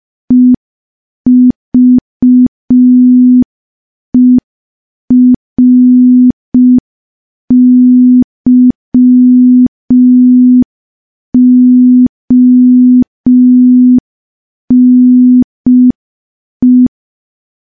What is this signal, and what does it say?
Morse code, heard as a keyed tone, about 5 wpm, 250 Hz. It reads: EVERYONE
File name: morse / morse-training-05WPM